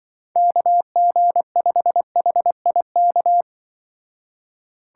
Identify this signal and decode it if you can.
Morse code, heard as a keyed tone, about 24 wpm, 700 Hz. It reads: KG5HIK
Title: Speed Racing - USA Call Signs 20wpm